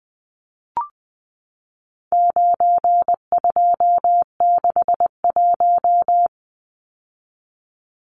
Morse code, heard as a keyed tone, about 20 wpm, 700 Hz. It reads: 9261
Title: Numbers - 4-Digit 20wpm